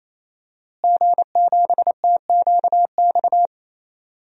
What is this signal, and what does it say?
Morse code, heard as a keyed tone, about 28 wpm, 700 Hz. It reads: G7TQX
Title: DXCC-100 Call Signs - 5 Characters - Repeated 3x - 28wpm